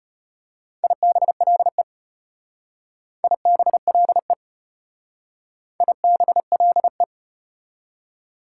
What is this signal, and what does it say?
Morse code, heard as a keyed tone, about 31 wpm, 700 Hz. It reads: IBLE IBLE IBLE
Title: Speed-Racing Prefixes and Suffixes 25wpm